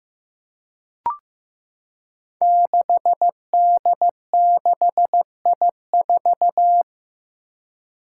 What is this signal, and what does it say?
Morse code, heard as a keyed tone, about 15 wpm, 700 Hz. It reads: 6D6I4
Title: Five Challenging Letter-Number Combinations 15wpm